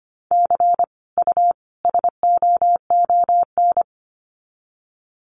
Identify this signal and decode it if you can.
Morse code, heard as a keyed tone, about 25 wpm, 700 Hz. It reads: C U SOON
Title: Common QSO Phrases 25wpm